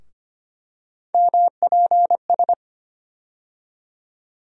Morse code, heard as a keyed tone, about 25 wpm, 700 Hz. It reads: MPS
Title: USA Call Sign Postfixes - 3 Letters 25wpm